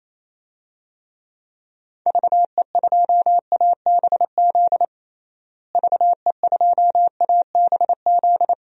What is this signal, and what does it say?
Morse code, heard as a keyed tone, about 28 wpm, 700 Hz. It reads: VE2ABZ VE2ABZ
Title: DXCC-100 Call Signs - 6 Characters - Repeated 2x - 28wpm